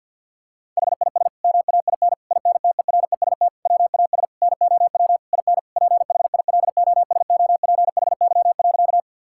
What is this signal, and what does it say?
Morse code, heard as a keyed tone, about 52 wpm, 700 Hz. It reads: HIS MAIN INTEREST WAS NOW IN PHILOSOPHY.
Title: Sentences from Top 1000 Words - Review Part 6 - 50wpm